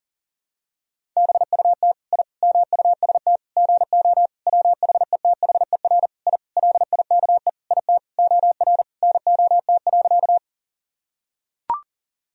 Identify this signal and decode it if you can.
Morse code, heard as a keyed tone, about 40 wpm, 700 Hz. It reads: BUT I MUST GO WHETHER I LIKE IT OR NOT.
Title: Sentences from Top 1000 Words - Review Part 3 - 40wpm